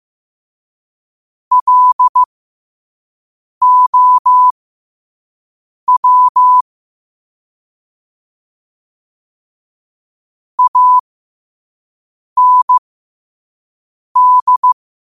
Morse code, heard as a keyed tone, about 15 wpm, spaced as for 5 wpm, 1000 Hz. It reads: LOW AND
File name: qotdfiles / 5wpm